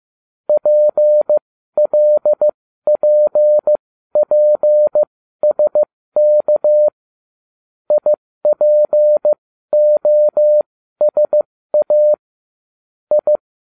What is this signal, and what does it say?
Morse code, heard as a keyed tone, about 15 wpm, spaced as for 12 wpm, 600 Hz. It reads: PLPPSK IPOSA I